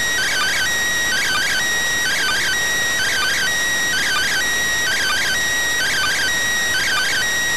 Two-tone teleprinter signal, about 400 Hz shift.